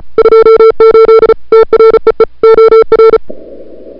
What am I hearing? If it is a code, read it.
Morse code, heard as a keyed tone, about 36 wpm, 440 Hz. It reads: 28 TREE OR
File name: n6tr / Fab2007CWSprintHour1